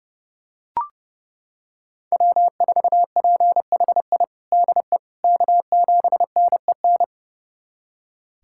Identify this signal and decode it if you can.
Morse code, heard as a keyed tone, about 32 wpm, 700 Hz. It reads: W4PHI DE K7NEN
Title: USA Callsign DE Callsign - 5 Characters - 30wpm